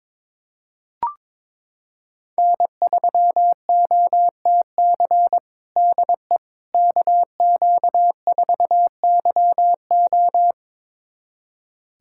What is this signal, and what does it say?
Morse code, heard as a keyed tone, about 22 wpm, 700 Hz. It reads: N3OTC DE KQ4YO
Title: USA Callsign DE Callsign - 5 Characters - 22wpm